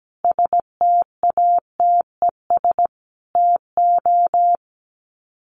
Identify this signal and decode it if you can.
Morse code, heard as a keyed tone, about 17 wpm, 700 Hz. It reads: STATES TO
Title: Sets of 5 Words Encore 17wpm